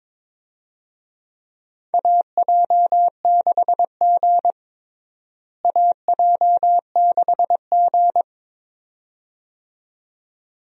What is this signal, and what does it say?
Morse code, heard as a keyed tone, about 22 wpm, 700 Hz. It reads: AJ6G AJ6G
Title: USA Call Signs - 4 Characters - Repeated 2x - 22wpm